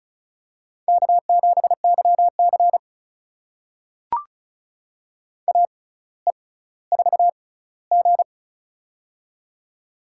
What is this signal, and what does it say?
Morse code, heard as a keyed tone, about 36 wpm, spaced as for 18 wpm, 700 Hz. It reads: K7YC AE4G
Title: USA Call Signs - 4 Characters - 6x Character Spacing - 35wpm